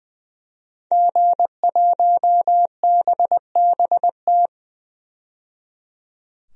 Morse code, heard as a keyed tone, about 21 wpm, 700 Hz. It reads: G1BBT
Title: DXCC-100 Call Signs - 5 Characters - 20wpm